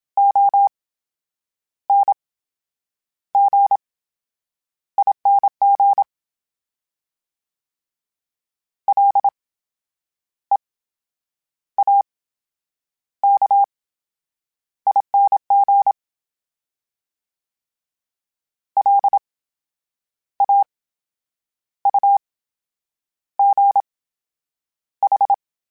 Morse code, heard as a keyed tone, about 27 wpm, spaced as for 8 wpm, 800 Hz. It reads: ONGING LEAKING LAUGH
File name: Special INGs / Special INGs-mp3s / VerySlowToFaster Ing @25